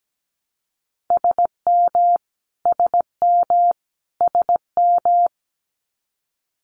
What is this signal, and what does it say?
Morse code, heard as a keyed tone, about 17 wpm, 700 Hz. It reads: SM SM SM
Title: Multi-letter Phonemes Repeated 3x 17wpm